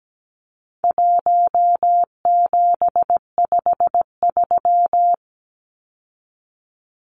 Morse code, heard as a keyed tone, about 17 wpm, 700 Hz. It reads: 1753
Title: Numbers - 4-Digit 17wpm